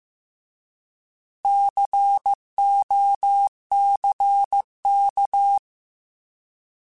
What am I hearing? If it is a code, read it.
Morse code, heard as a keyed tone, about 15 wpm, 800 Hz. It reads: COCK
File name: Alphabetical / Alphabetical mp3s / Cs @13